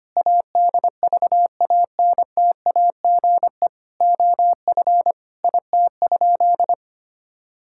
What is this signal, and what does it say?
Morse code, heard as a keyed tone, about 25 wpm, 700 Hz. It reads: ADVANTAGE OF IT?
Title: Sentences from Top 1000 Words - Review Part 6 - 25wpm